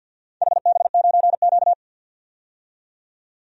Morse code, heard as a keyed tone, about 46 wpm, 700 Hz. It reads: HB9Q
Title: DXCC-100 Call Signs - 4 Characters - 50wpm